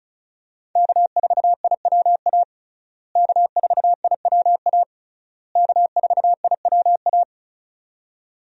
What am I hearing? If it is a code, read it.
Morse code, heard as a keyed tone, about 36 wpm, 700 Hz. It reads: K4IWA K4IWA K4IWA
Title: USA Call Signs - 5 Characters - Repeated 3x - 35wpm